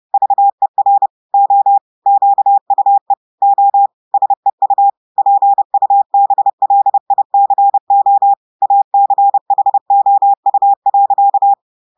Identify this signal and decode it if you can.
Morse code, heard as a keyed tone, about 30 wpm, 800 Hz. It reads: VER O QUE O SEU PUBLICO ACHOU.
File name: audios / portugais / portugais texte en clair 2 vitesse 30 mots minutes